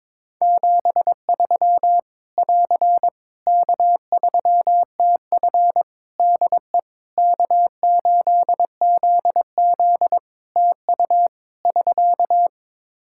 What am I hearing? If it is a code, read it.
Morse code, heard as a keyed tone, about 22 wpm, 700 Hz. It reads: 73 <AR> K3TF DE K8ZZ TU <SK>
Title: K1USN SST QSO Practice 22wpm